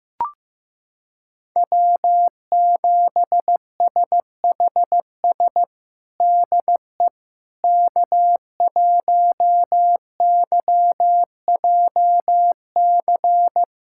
Morse code, heard as a keyed tone, about 15 wpm, 700 Hz. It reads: W7SHS DE K1YJC
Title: USA Callsign DE Callsign - 5 Characters - 15wpm